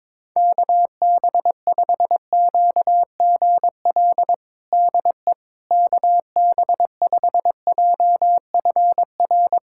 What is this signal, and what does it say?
Morse code, heard as a keyed tone, about 22 wpm, 700 Hz. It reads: KB5QGL DE KB5JFR
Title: USA Callsign DE Callsign - 6 Characters - 22wpm